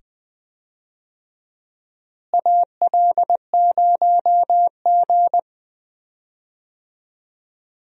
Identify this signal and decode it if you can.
Morse code, heard as a keyed tone, about 20 wpm, 700 Hz. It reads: AL0G
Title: USA Call Signs - 4 Characters - Repeated 2x - 20wpm